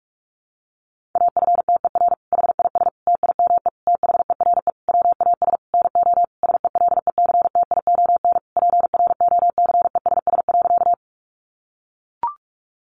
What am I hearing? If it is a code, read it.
Morse code, heard as a keyed tone, about 44 wpm, 700 Hz. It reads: AFTER HIS TIME THERE WAS NO SELECTION PROCESS.
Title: Sentences from Top 1400 Words 45wpm